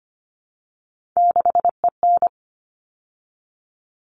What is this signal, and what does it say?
Morse code, heard as a keyed tone, about 25 wpm, 700 Hz. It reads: 6EN
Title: All Cut Numbers - 3-Digit 25wpm